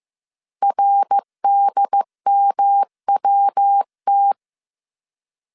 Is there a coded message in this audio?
Yes: Morse code from 0.6 to 4.3 s, heard as a keyed tone, about 15 wpm, 800 Hz. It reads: RDMWT